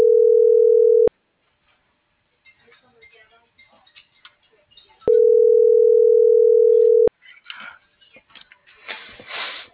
Ringback tone (North America), 440 + 480 Hz.